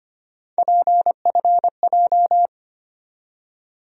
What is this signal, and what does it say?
Morse code, heard as a keyed tone, about 25 wpm, 700 Hz. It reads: PFJ
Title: USA Call Sign Postfixes - 3 Letters 25wpm